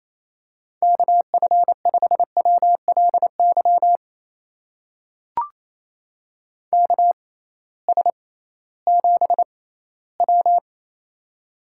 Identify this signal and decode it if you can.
Morse code, heard as a keyed tone, about 29 wpm, spaced as for 17 wpm, 700 Hz. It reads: KF5WLY KS7W
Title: USA Call Signs - 6x Character Spacing - 28wpm